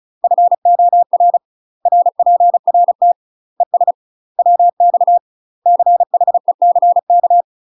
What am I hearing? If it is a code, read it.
Morse code, heard as a keyed tone, about 36 wpm, 700 Hz. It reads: FOR RPRT ES WX CHECK
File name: UK MASTER QSOs TRAINING / 35WPM / UK MASTER QSOs 0032